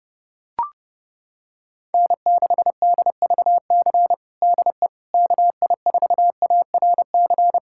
Morse code, heard as a keyed tone, about 30 wpm, 700 Hz. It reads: N6DVC DE KI4ARC
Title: USA Callsign DE Callsign 30wpm